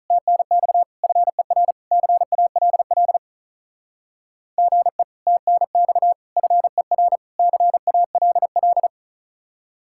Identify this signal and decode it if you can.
Morse code, heard as a keyed tone, about 45 wpm, 700 Hz. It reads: TNX FER CALL GE TNX FER CALL